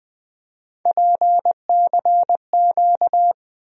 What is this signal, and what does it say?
Morse code, heard as a keyed tone, about 20 wpm, 700 Hz. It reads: PCQ
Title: USA Call Sign Postfixes - 3 Letters 20wpm